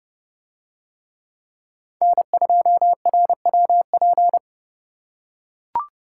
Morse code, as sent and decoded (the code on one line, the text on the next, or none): -. ..--- .-. .-- .--.
N2RWP